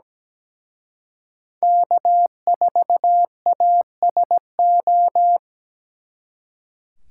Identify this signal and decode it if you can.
Morse code, heard as a keyed tone, about 20 wpm, 700 Hz. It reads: K4ASO
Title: Speed Racing - USA 1x3 Call Signs 17wpm